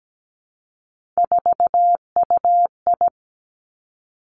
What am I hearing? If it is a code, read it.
Morse code, heard as a keyed tone, about 17 wpm, 700 Hz. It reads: 4UI
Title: Three Challenging Letter-Number Combinations 17wpm